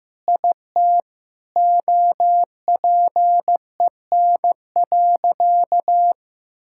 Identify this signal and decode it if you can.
Morse code, heard as a keyed tone, about 15 wpm, 700 Hz. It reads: IT OPEN.
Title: Sentences from Top 1000 Words - Review Part 6 - 15wpm